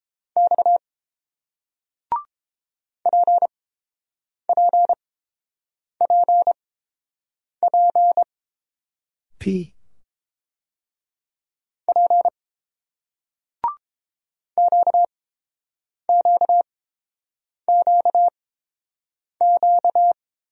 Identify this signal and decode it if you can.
Morse code, heard as a keyed tone, about 30 wpm, spaced as for 7 wpm, 690 Hz. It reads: X PPPP P QQQQ